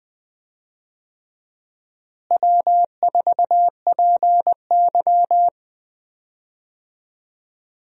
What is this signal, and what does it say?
Morse code, heard as a keyed tone, about 20 wpm, 700 Hz. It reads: W4PY
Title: USA Call Signs - 4 Characters - Repeated 3x - 20wpm